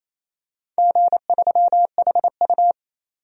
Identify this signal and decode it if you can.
Morse code, heard as a keyed tone, about 28 wpm, 700 Hz. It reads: G3HU
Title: DXCC-100 Call Signs - 4 Characters - 28wpm